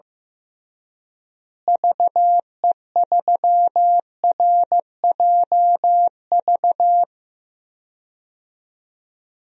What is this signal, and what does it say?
Morse code, heard as a keyed tone, about 15 wpm, 700 Hz. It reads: VE3RJV
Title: DXCC-100 Call Signs - 6 Characters - Repeated 2x - 15wpm